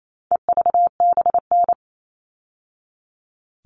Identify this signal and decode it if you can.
Morse code, heard as a keyed tone, about 28 wpm, 700 Hz. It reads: EVBN
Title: All Cut Numbers - 4-Digit 28wpm